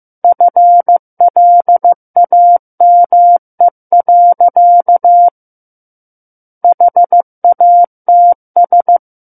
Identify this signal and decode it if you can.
Morse code, heard as a keyed tone, about 15 wpm, 700 Hz. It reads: FLAME. HATS
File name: Book One / 06-Chapter 6 - 15 WPM